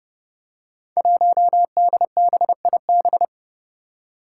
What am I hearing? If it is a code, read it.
Morse code, heard as a keyed tone, about 32 wpm, 700 Hz. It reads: 1DBIB
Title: Five Challenging Letter-Number Combinations 30wpm